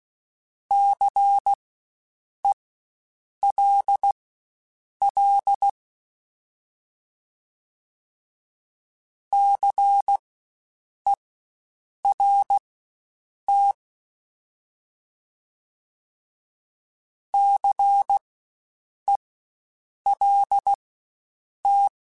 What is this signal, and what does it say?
Morse code, heard as a keyed tone, about 16 wpm, spaced as for 7 wpm, 800 Hz. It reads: CELL CERT CELT
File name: Spaced Alphas / Spaced Alphas mp3s / Cs sp @15